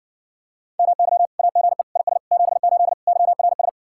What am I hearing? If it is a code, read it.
Morse code, heard as a keyed tone, about 61 wpm, 700 Hz. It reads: MY AGE IS 78 YRS